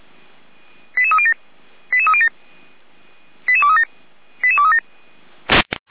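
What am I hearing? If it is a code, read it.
Five-tone selective call (ZVEI), heard as quick runs of five tones, tones of about 70 ms. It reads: 88307 88307 88237 88237
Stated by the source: ZVEI five-tone call received by SDR, recorded by Feuerwehr Orlamünde.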